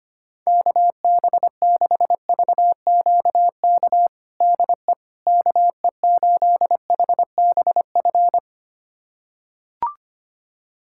Morse code, heard as a keyed tone, about 25 wpm, 700 Hz. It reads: KB6VQK DE KE8HBF
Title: USA Callsign DE Callsign - 6 Characters - 25wpm